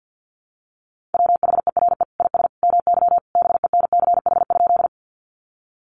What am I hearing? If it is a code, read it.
Morse code, heard as a keyed tone, about 50 wpm, 700 Hz. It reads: WHERE IS MY BENCH?
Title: Sentences from Top 1400 Words 50wpm